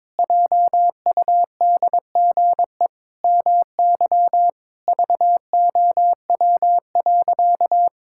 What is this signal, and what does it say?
Morse code, heard as a keyed tone, about 22 wpm, 700 Hz. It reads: JUDGE MY VOW.
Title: Pangram Challenge v2 - 22wpm